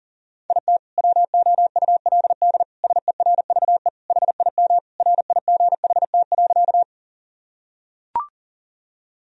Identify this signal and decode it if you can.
Morse code, heard as a keyed tone, about 40 wpm, 700 Hz. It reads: IT WOULD SERVE HIM RIGHT.
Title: Sentences from Top 1000 Words - Review Part 2 - 40wpm